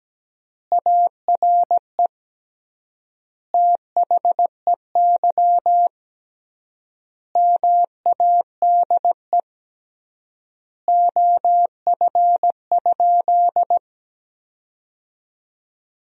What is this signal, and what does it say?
Morse code, heard as a keyed tone, about 17 wpm, 700 Hz. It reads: ARE THEY MADE OF?
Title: 3x Word-Spacing - Sentences from Top 200 Words - 17wpm